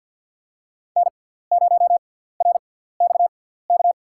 Morse code, heard as a keyed tone, about 53 wpm, spaced as for 17 wpm, 700 Hz. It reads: N0RXK
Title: USA Call Signs - 5 Characters - 6x Character Spacing - 50wpm